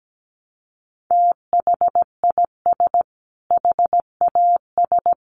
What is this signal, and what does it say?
Morse code, heard as a keyed tone, about 17 wpm, 700 Hz. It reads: THIS HAS
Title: Sentences from Top 1400 Words 17wpm